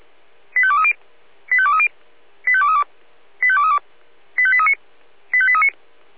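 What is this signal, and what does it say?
Five-tone selective call (ZVEI), heard as quick runs of five tones, tones of about 70 ms. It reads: 85230 85230 85232 85232 85839 85839
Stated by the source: ZVEI five-tone selective call 85855, received on an SDR.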